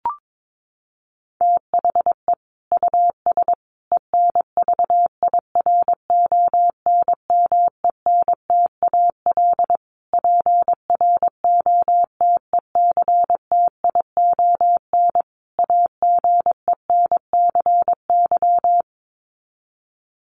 Morse code, heard as a keyed tone, about 22 wpm, 700 Hz. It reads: THE US ENVIRONMENTAL PROTECTION AGENCY
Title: Sets of 5 Words Encore2 22wpm